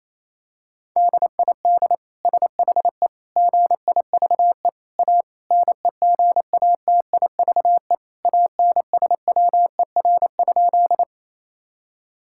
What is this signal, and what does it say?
Morse code, heard as a keyed tone, about 27 wpm, 700 Hz. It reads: DID SHE GIVE A NEGATIVE ANSWER?
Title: Sentences from Top 1000 Words - Review Part 6 - 28wpm